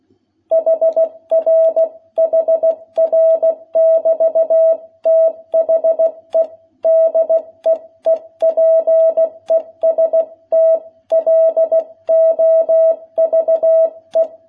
Morse code, heard as a keyed tone, about 16 wpm, 660 Hz. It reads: HR HR <BT> THE DEEPEST LOVE
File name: jh8cbh / 2021cw